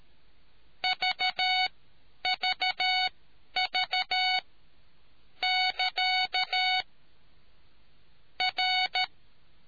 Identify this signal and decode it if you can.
Morse code, heard as a keyed tone, about 13 wpm, spaced as for 10 wpm, 740 Hz. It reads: VVV <KA> R